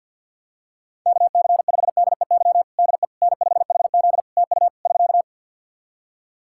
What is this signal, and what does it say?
Morse code, heard as a keyed tone, about 50 wpm, 700 Hz. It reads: KC5DEY DE N5HZ TU <SK>
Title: Practice QSO - Exchange 4 - Ending - 50wpm